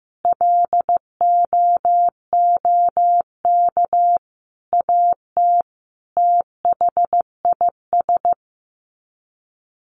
Morse code, heard as a keyed tone, about 15 wpm, 700 Hz. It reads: LOOK AT THIS
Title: Sets of 5 Words 15wpm